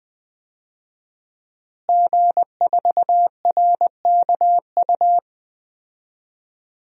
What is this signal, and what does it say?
Morse code, heard as a keyed tone, about 20 wpm, 700 Hz. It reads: G4RKU
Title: DXCC-100 Call Signs - 5 Characters - Repeated 2x - 20wpm